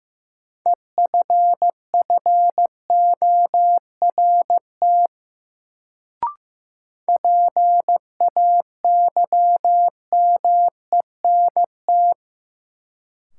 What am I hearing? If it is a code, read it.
Morse code, heard as a keyed tone, about 15 wpm, 700 Hz. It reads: EFFORT PAYMENT
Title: Sentences from Top 1200 Words - New Word Drill - 15wpm